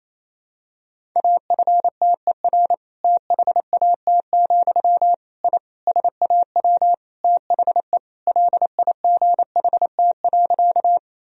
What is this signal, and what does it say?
Morse code, heard as a keyed tone, about 28 wpm, 700 Hz. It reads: AFTER THAT, I SAW THE LIGHT.
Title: Sentences from Top 1000 Words - Review Part 1 - 28wpm